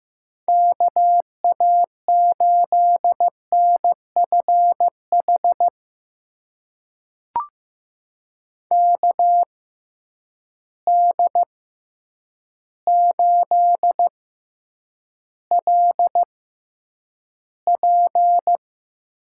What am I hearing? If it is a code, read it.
Morse code, heard as a keyed tone, about 15 wpm, spaced as for 8 wpm, 700 Hz. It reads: KA8NFH KD8LP
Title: USA Call Signs - 6 Characters - 6x Character Spacing - 15wpm